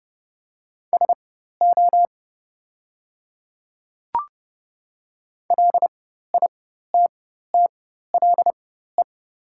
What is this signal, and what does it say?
Morse code, heard as a keyed tone, about 30 wpm, spaced as for 14 wpm, 700 Hz. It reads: SO LITTLE
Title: 4x Character Spacing - Top 200 Words - 30wpm